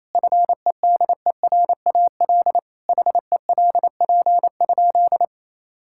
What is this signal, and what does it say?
Morse code, heard as a keyed tone, about 27 wpm, 700 Hz. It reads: FEDERAL HELP?
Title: Sentences from Top 1000 Words - Review Part 5 - 28wpm